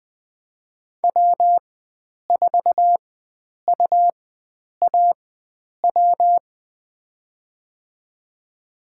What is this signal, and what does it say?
Morse code, heard as a keyed tone, about 20 wpm, spaced as for 9 wpm, 700 Hz. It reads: W4UAW
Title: USA Call Signs - 5 Characters - 4x Character Spacing - 20wpm